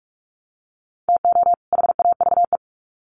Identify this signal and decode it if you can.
Morse code, heard as a keyed tone, about 44 wpm, 700 Hz. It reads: TO HAVE
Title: Sets of 2 Words 45wpm